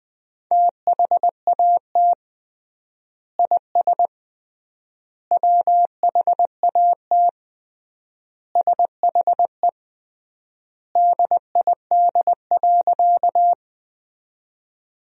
Morse code, heard as a keyed tone, about 21 wpm, 700 Hz. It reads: THAT IS WHAT SHE DID.
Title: 3x Word-Spacing - Sentences from Top 100 Words - 20wpm